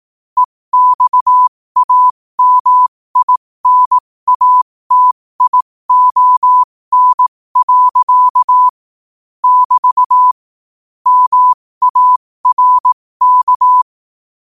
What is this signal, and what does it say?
Morse code, heard as a keyed tone, about 18 wpm, spaced as for 15 wpm, 1000 Hz. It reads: EXAMINATION. <BT> MARK
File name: 15wpm